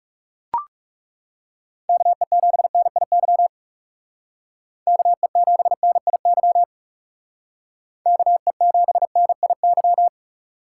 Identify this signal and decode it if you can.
Morse code, heard as a keyed tone, about 39 wpm, 700 Hz. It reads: KE7NIY KE7NIY KE7NIY